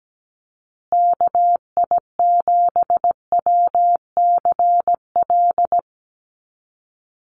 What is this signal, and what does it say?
Morse code, heard as a keyed tone, about 17 wpm, 700 Hz. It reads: KI7WCL
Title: USA Call Signs Repeated 3x 17wpm